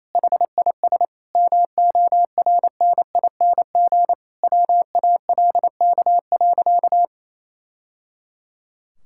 Morse code, heard as a keyed tone, about 28 wpm, 700 Hz. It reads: HIS MORNING WALK.
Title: Sentences from Top 1000 Words - Review Part 5 - 28wpm